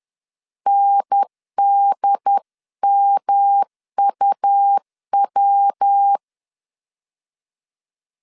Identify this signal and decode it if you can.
Morse code, heard as a keyed tone, about 10 wpm, 800 Hz. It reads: NDMUW
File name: CW / DET / 1 3 1500C S C 10wpm 3335